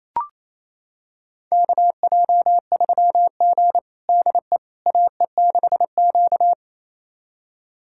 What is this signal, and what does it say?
Morse code, heard as a keyed tone, about 28 wpm, 700 Hz. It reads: KJ3G DE AE6Q
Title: USA Callsign DE Callsign - 4 Characters - 28wpm